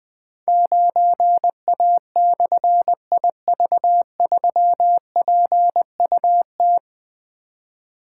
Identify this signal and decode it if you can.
Morse code, heard as a keyed tone, about 20 wpm, 700 Hz. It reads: 9A/IV3PUT
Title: Difficult Call Signs 20wpm